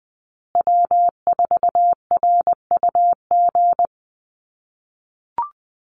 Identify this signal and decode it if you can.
Morse code, heard as a keyed tone, about 20 wpm, 700 Hz. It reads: W4RUG
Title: USA Call Signs Repeated 2x 20wpm